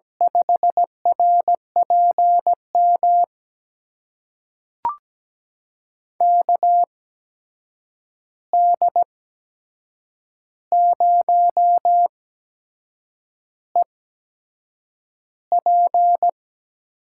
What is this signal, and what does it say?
Morse code, heard as a keyed tone, about 17 wpm, spaced as for 7 wpm, 700 Hz. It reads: 5RPM KD0EP
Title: USA Call Signs - 8x Character Spacing - 17wpm